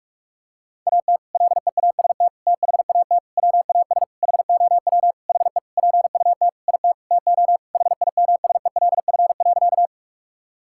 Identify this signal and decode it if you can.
Morse code, heard as a keyed tone, about 47 wpm, 700 Hz. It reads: AT LEAST THAT WAS HOW HE PUT IT TO HIMSELF.
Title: Sentences from Top 1600 Words 45wpm